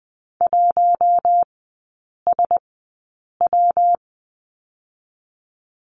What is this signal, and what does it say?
Morse code, heard as a keyed tone, about 20 wpm, 700 Hz. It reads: 1 S W
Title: 114 Lesson 19e - Letter w - Three Characters 20wpm